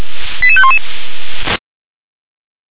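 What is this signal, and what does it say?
Five-tone selective call (ZVEI), heard as quick runs of five tones, tones of about 70 ms. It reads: 88511
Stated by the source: ZVEI five-tone selective call received by SDR.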